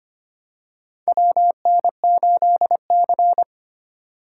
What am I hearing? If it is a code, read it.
Morse code, heard as a keyed tone, about 25 wpm, 700 Hz. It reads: WN8C
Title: USA Call Signs - 4 Characters - 3x Character Spacing - 25wpm